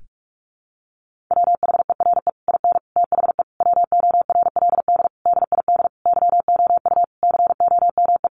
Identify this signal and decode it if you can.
Morse code, heard as a keyed tone, about 44 wpm, 700 Hz. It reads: WHERE IN THE WORLD DID YOU COME
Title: Sentences from Top 200 Words 45wpm